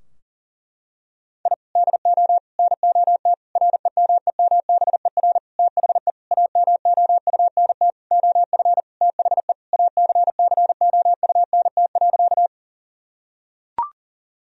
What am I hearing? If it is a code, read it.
Morse code, heard as a keyed tone, about 37 wpm, 700 Hz. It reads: I DO NOT REMEMBER THE AMOUNT OF THE ACCOUNT.
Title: Sentences from Top 1000 Words - Review Part 4 - 40wpm